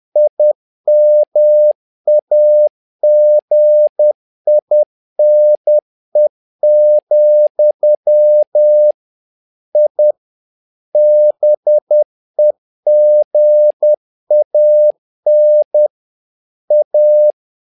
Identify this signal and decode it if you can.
Morse code, heard as a keyed tone, about 10 wpm, 600 Hz. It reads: IMAGINE, I BEGAN A
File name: learning center / tower trouble / tower trouble 10wpm